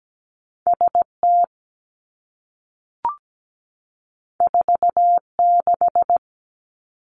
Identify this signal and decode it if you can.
Morse code, heard as a keyed tone, about 17 wpm, 700 Hz. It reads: S T 4 6
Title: Two Challenging Letter-Number Combinations 17wpm